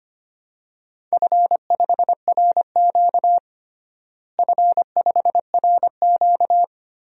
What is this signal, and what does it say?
Morse code, heard as a keyed tone, about 25 wpm, 700 Hz. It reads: F5RQ F5RQ
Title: DXCC-100 Call Signs - 4 Characters - Repeated 3x - 25wpm